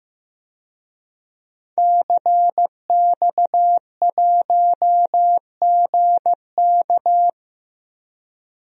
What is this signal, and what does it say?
Morse code, heard as a keyed tone, about 16 wpm, 700 Hz. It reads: CX1GK
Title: DXCC-100 Call Signs - 5 Characters - 15wpm